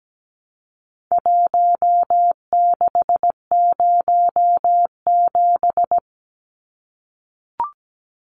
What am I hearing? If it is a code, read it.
Morse code, heard as a keyed tone, about 17 wpm, 700 Hz. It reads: 1607
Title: Numbers - Contest Style - 4-Digit 17wpm